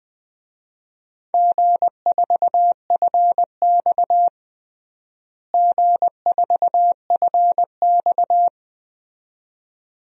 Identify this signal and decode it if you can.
Morse code, heard as a keyed tone, about 21 wpm, 700 Hz. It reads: G4FX G4FX
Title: DXCC-100 Call Signs - 4 Characters - Repeated 2x - 20wpm